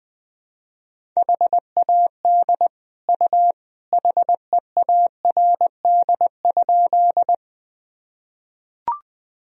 Morse code, heard as a keyed tone, about 21 wpm, 700 Hz. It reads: HAD U HEARD?
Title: Sentences from Top 400 Words and Abbreviations 20wpm